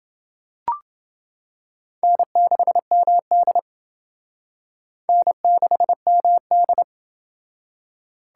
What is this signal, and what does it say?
Morse code, heard as a keyed tone, about 28 wpm, 700 Hz. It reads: N6MD N6MD